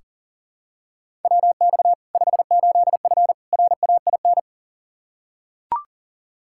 Morse code, heard as a keyed tone, about 40 wpm, 700 Hz. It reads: WX 58F RAIN
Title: Practice QSO - Exchange 2 - Info - 40wpm